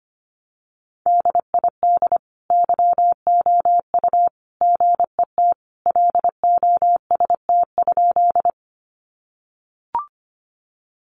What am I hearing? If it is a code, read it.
Morse code, heard as a keyed tone, about 25 wpm, 700 Hz. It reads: DID YOU GET LOST?
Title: Sentences from Top 1400 Words 25wpm